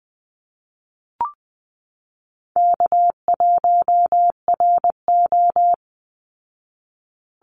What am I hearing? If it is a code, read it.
Morse code, heard as a keyed tone, about 20 wpm, 700 Hz. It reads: K1RO
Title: Easy Call Signs Repeated 3x - 20wpm